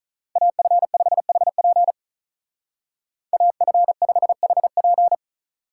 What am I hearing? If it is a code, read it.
Morse code, heard as a keyed tone, about 47 wpm, 700 Hz. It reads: AF5HP AF5HP